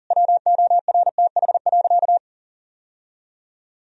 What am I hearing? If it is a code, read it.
Morse code, heard as a keyed tone, about 37 wpm, 700 Hz. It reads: WORTH.
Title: Sentences from Top 1000 Words - Review Part 4 - 40wpm